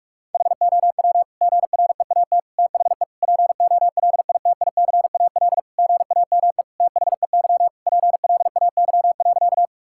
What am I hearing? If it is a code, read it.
Morse code, heard as a keyed tone, about 46 wpm, 700 Hz. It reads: HOW GREAT THE POLITICAL GAME THEY PLAY.
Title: Sentences from Top 1000 Words - Review Part 5 - 45wpm